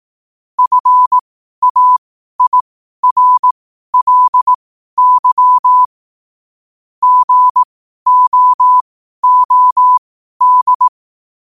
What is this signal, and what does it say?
Morse code, heard as a keyed tone, about 18 wpm, spaced as for 12 wpm, 1000 Hz. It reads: FAIRLY GOOD
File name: qotdfiles / 12wpm